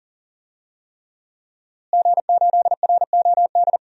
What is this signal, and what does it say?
Morse code, heard as a keyed tone, about 40 wpm, 700 Hz. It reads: G8ROD
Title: DXCC-100 Call Signs - 5 Characters - Repeated 3x - 40wpm